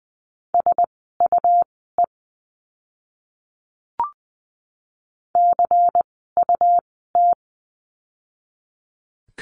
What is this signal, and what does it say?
Morse code, heard as a keyed tone, about 20 wpm, spaced as for 15 wpm, 700 Hz. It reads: SUE CUT
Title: 101 Lesson 17f - Letter u - Words 20wpm